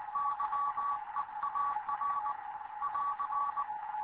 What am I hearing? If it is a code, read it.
Morse code, heard as a keyed tone, about 20 wpm, 1140 Hz, with radio noise. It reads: YEAR <AR>
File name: Morze-HA2MN / hangp / 2 11 GYAKORLAT